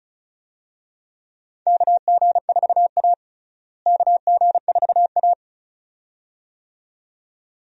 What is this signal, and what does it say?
Morse code, heard as a keyed tone, about 34 wpm, 700 Hz. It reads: KG4A KG4A
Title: USA Call Signs - 4 Characters - Repeated 2x - 35wpm